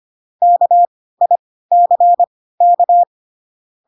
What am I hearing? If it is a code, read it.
Morse code, heard as a keyed tone, about 25 wpm, spaced as for 16 wpm, 700 Hz. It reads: KICK
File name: Pinocchio / Pinocchio ch3 10 25 ik1hge